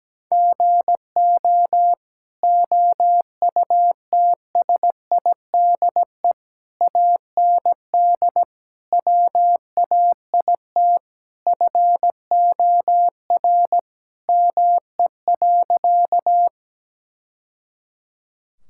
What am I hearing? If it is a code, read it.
Morse code, heard as a keyed tone, about 17 wpm, 700 Hz. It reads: GO OUTSIDE AND WAIT FOR ME.
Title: Sentences from Top 1000 Words - Review Part 6 - 17wpm